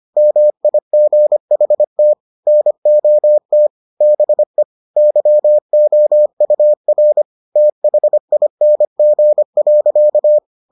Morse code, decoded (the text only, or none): MIGHT NOT BE YOUR THING.